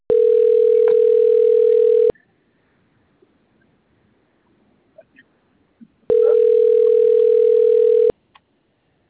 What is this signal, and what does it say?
Ringback tone (North America), 440 + 480 Hz.